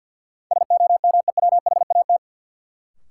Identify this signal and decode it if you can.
Morse code, heard as a keyed tone, about 53 wpm, 700 Hz. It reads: SOMEWHAT
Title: Sentences from Top 1000 Words - Review Part 1 - New Word Drill - 50wpm